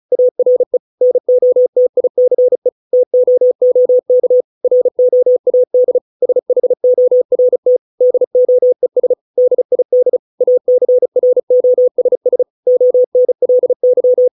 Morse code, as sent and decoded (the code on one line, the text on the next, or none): .- .-. / -. --- - .. -.-. / - --- --- -.- / .-. --- .- -.. / ... .... --- .-. - / -.. --- . ... / -.. .. -.. / .- -.-. .-. --- ... ... / --- -. .-.. -.--
ARE NOTICE TOOK ROAD SHORT DOES DID ACROSS ONLY